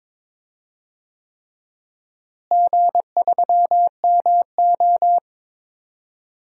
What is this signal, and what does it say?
Morse code, heard as a keyed tone, about 21 wpm, 700 Hz. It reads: G3MO
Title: DXCC-100 Call Signs - 4 Characters - Repeated 3x - 22wpm